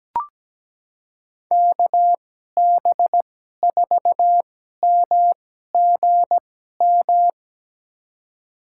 Morse code, heard as a keyed tone, about 17 wpm, spaced as for 12 wpm, 700 Hz. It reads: KB4MGM
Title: USA Call Signs - 2x Character Spacing - 17wpm